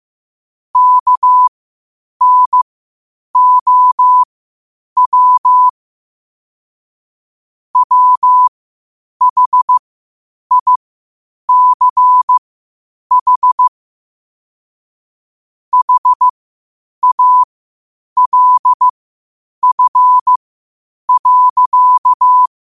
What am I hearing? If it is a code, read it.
Morse code, heard as a keyed tone, about 15 wpm, spaced as for 8 wpm, 1000 Hz. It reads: KNOW WHICH HALF.
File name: qotdfiles / 8wpm